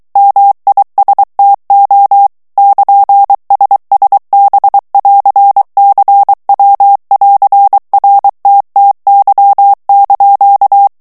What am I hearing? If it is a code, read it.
Morse code, heard as a keyed tone, about 23 wpm, 800 Hz. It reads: MISTO <KN>SSB<AR>CW<AR>RTTY)
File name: ITALIEN texte clair 2 vitesse 22 mots minute